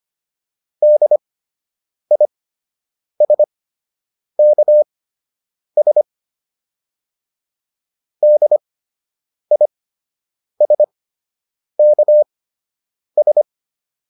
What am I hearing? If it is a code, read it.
Morse code, heard as a keyed tone, about 25 wpm, spaced as for 8 wpm, 600 Hz. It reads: DISKS DISKS